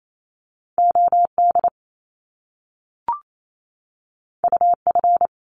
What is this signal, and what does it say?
Morse code, heard as a keyed tone, about 28 wpm, 700 Hz. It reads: OD UF
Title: Top Two Letter Combinations in Words 28wpm